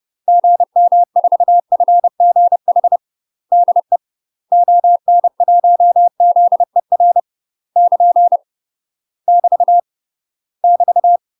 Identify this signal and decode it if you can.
Morse code, heard as a keyed tone, about 30 wpm, 700 Hz. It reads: GM4FGH DE ON1ZER <KN> <BT> <BT>
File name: UK MASTER QSOs TRAINING / 30WPM / UK MASTER QSOs 0029